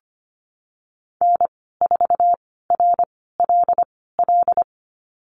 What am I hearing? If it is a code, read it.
Morse code, heard as a keyed tone, about 25 wpm, spaced as for 16 wpm, 700 Hz. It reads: N4RLL
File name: Lesson 15g - Number 5 - Call Signs 25wpm